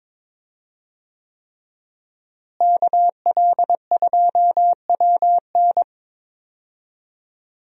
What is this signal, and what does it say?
Morse code, heard as a keyed tone, about 22 wpm, 700 Hz. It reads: KL2WN
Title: USA Call Signs - 5 Characters - Repeated 3x - 22wpm